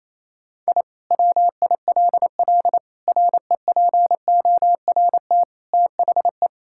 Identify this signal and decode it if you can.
Morse code, heard as a keyed tone, about 28 wpm, 700 Hz. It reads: I WILL REPORT THE
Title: Sentences from Top 1000 Words - Review Part 6 - 28wpm